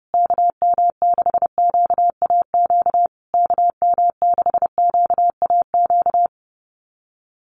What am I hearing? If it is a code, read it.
Morse code, heard as a keyed tone, about 30 wpm, 700 Hz. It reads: KM6QAQ KM6QAQ
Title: USA Call Signs Repeated 3x 30wpm